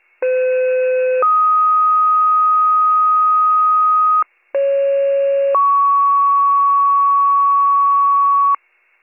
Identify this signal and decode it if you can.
Two-tone page, heard as a one-second tone, then a three-second tone, about 510 then 1250 Hz, about 550 then 1120 Hz. It is heard from 0.2 to 8.6 s.